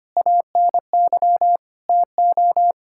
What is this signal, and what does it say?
Morse code, heard as a keyed tone, about 25 wpm, 700 Hz. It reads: ANY TO
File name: practice / 25wpm / Sentences with 9-Word Phrases - 25wpm